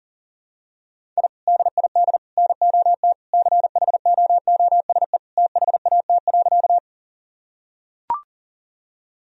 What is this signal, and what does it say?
Morse code, heard as a keyed tone, about 40 wpm, 700 Hz. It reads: I DID NOT CHOOSE THAT.
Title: Sentences from Top 1000 Words - Review Part 4 - 40wpm